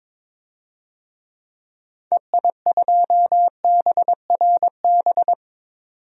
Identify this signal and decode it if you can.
Morse code, heard as a keyed tone, about 21 wpm, 700 Hz. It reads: EI2BRB
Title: DXCC-100 Call Signs - 6 Characters - 22wpm